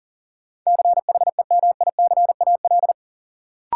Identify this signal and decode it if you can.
Morse code, heard as a keyed tone, about 40 wpm, 700 Hz. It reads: CHEMICAL
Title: Sentences from Top 1200 Words - New Word Drill - 40wpm